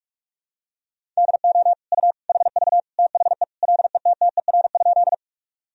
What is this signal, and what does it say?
Morse code, heard as a keyed tone, about 41 wpm, 700 Hz. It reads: DO U HV THE LETTER?
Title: Sentences from Top 300 Words and Abbreviations 45wpm